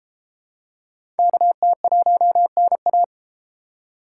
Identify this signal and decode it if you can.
Morse code, heard as a keyed tone, about 30 wpm, 700 Hz. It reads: KT1NA